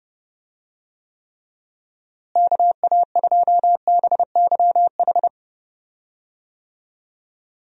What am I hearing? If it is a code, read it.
Morse code, heard as a keyed tone, about 30 wpm, 700 Hz. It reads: KA2BYH
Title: USA Call Signs - 6 Characters - Repeated 2x - 30wpm